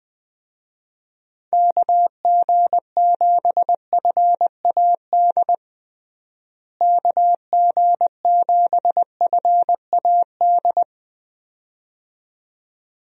Morse code, heard as a keyed tone, about 20 wpm, 700 Hz. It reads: KG7FAD KG7FAD
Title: USA Call Signs - 6 Characters - Repeated 2x - 20wpm